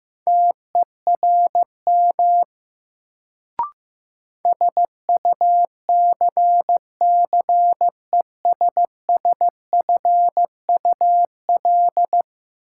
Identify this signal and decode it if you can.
Morse code, heard as a keyed tone, about 15 wpm, 700 Hz. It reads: TERM SUCCESSFUL
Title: Sentences from Top 900 Words - New Word Drill - 15wpm